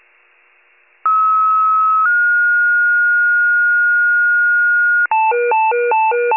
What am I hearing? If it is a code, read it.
Two-tone page, heard as a one-second tone, then a three-second tone, about 1290 then 1430 Hz. It is heard from 1.0 to 5.1 s.